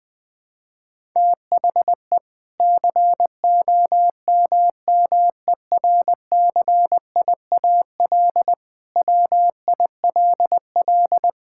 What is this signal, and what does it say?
Morse code, heard as a keyed tone, about 21 wpm, 700 Hz. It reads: THE COMMERCIAL WILL
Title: Sentences from Top 1000 Words - Review Part 5 - 20wpm